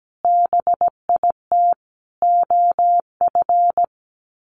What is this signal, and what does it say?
Morse code, heard as a keyed tone, about 17 wpm, 700 Hz. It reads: BIT OF
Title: Sets of 4 Words 17wpm